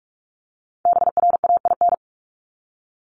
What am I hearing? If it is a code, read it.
Morse code, heard as a keyed tone, about 44 wpm, 700 Hz. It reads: BRAIN
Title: Top 1001-1250 Words 45wpm